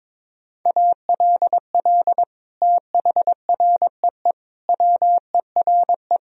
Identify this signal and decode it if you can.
Morse code, heard as a keyed tone, about 22 wpm, 700 Hz. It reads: ALL THREE WERE
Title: Sentences from Top 1000 Words - Review Part 4 - 22wpm